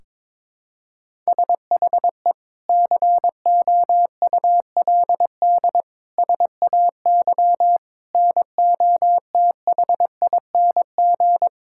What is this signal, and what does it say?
Morse code, heard as a keyed tone, about 22 wpm, 700 Hz. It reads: SHE COULD SAY NOTHING
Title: Sentences from Top 1600 Words 22wpm